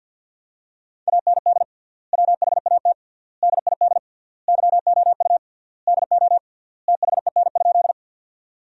Morse code, heard as a keyed tone, about 50 wpm, 700 Hz. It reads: AND WHAT DID YOU DO THEN?
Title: 3x Word-Spacing - Sentences from Top 100 Words - 50wpm